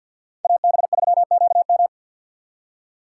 Morse code, heard as a keyed tone, about 52 wpm, 700 Hz. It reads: AB3QM
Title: USA Call Signs - 5 Characters - 6x Character Spacing - 50wpm